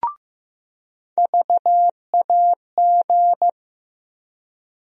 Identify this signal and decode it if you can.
Morse code, heard as a keyed tone, about 15 wpm, 700 Hz. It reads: VAG
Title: USA Call Sign Postfixes - 3 Letters 15wpm